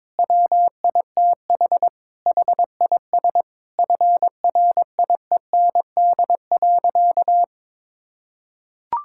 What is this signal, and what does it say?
Morse code, heard as a keyed tone, about 22 wpm, 700 Hz. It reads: WITH HIS FRIEND.
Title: Sentences from Top 1600 Words 22wpm